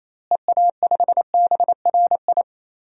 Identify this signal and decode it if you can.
Morse code, heard as a keyed tone, about 28 wpm, 700 Hz. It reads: EA5BRI
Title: DXCC-100 Call Signs - 6 Characters - Repeated 3x - 28wpm